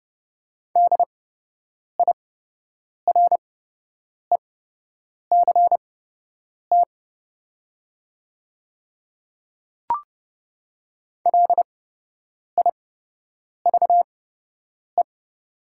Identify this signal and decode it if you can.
Morse code, heard as a keyed tone, about 30 wpm, spaced as for 8 wpm, 700 Hz. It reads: DIRECT LIVE
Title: 8x Character Spacing - Top 500 Words - 30wpm